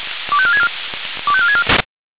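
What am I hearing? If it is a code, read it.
Five-tone selective call (ZVEI), heard as quick runs of five tones, tones of about 80 ms. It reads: 25464 25464
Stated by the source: ZVEI five-tone selective call 25459 received on SDR.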